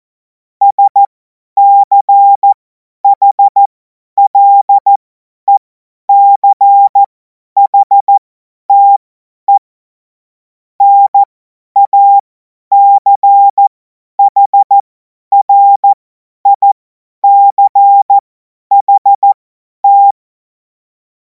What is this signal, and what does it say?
Morse code, heard as a keyed tone, about 14 wpm, spaced as for 10 wpm, 800 Hz. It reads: SCHLECHTE NACHRICHT
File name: Morsekurs / Witze / Witz 14. Juli 2026 50 F70